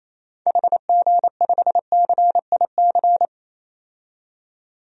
Morse code, heard as a keyed tone, about 28 wpm, 700 Hz. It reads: HG5CIC
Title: DXCC-100 Call Signs - 6 Characters - 28wpm